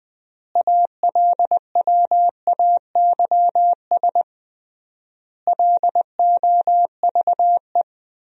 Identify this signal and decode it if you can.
Morse code, heard as a keyed tone, about 20 wpm, 700 Hz. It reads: ALWAYS LOVE
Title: 3x Word-Spacing - Sentences from Top 400 Words - 20wpm